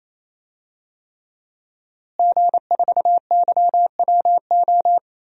Morse code, heard as a keyed tone, about 28 wpm, 700 Hz. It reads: G4YWO
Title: DXCC-100 Call Signs - 5 Characters - Repeated 3x - 28wpm